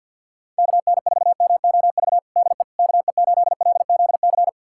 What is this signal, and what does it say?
Morse code, heard as a keyed tone, about 53 wpm, 700 Hz. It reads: KN4MOV DE KE8LZC
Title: USA Callsign DE Callsign - 6 Characters - 50wpm